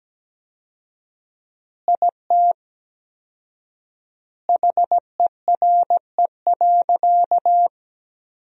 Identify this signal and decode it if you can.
Morse code, heard as a keyed tone, about 17 wpm, 700 Hz. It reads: IT HERE.
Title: 4x Word-Spacing - Sentences from Top 500 Words - 17wpm